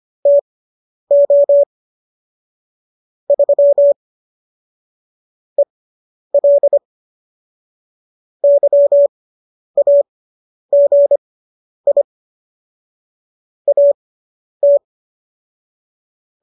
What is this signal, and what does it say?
Morse code, heard as a keyed tone, about 25 wpm, spaced as for 10 wpm, 570 Hz. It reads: TO 3 EL YAGI AT